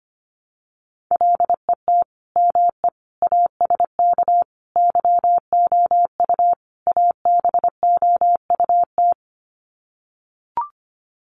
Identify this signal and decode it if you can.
Morse code, heard as a keyed tone, about 25 wpm, 700 Hz. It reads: LET ME ASK YOU ABOUT
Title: Sets of 5 Words 25wpm